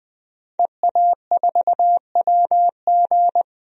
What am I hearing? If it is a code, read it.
Morse code, heard as a keyed tone, about 20 wpm, 700 Hz. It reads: EA4WG
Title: DXCC-100 Call Signs - 5 Characters - Repeated 3x - 20wpm